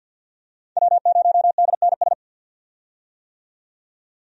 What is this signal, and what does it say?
Morse code, heard as a keyed tone, about 49 wpm, 700 Hz. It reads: W0DNS
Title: USA Call Signs - 5 Characters - 50wpm